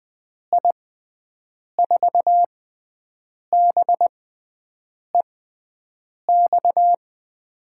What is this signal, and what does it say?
Morse code, heard as a keyed tone, about 20 wpm, spaced as for 7 wpm, 700 Hz. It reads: I4BEX
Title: USA Call Signs - 6 Characters - 6x Character Spacing - 20wpm